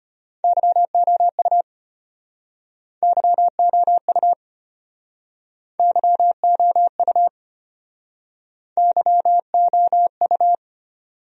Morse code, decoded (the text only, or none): YOU YOU YOU YOU